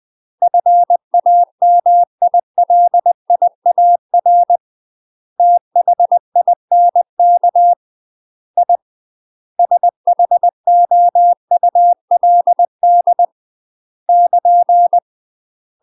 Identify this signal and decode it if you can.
Morse code, heard as a keyed tone, about 20 wpm, 700 Hz. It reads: FAMILIAR THINK I SHOULD <KN>
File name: UK MASTER QSOs TRAINING / 20WPM / UK MASTER QSOs 0030